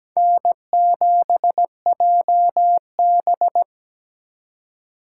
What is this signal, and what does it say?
Morse code, heard as a keyed tone, about 17 wpm, 700 Hz. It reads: N7JB
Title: USA Call Signs - 4x Character Spacing - 17wpm